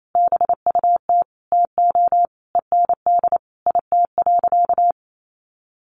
Morse code, heard as a keyed tone, about 28 wpm, 700 Hz. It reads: BUT TO END IT.
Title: Sentences from Top 1200 Words 28wpm